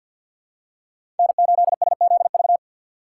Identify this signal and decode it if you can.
Morse code, heard as a keyed tone, about 53 wpm, 700 Hz. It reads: N8SZV
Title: USA Call Signs - 5 Characters - 3x Character Spacing - 50wpm